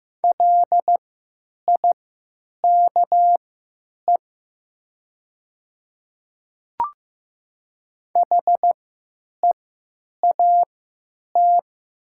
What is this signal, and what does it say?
Morse code, heard as a keyed tone, about 15 wpm, spaced as for 9 wpm, 700 Hz. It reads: LIKE HEAT
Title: 3x Character Spacing - Top 500 Words - 15wpm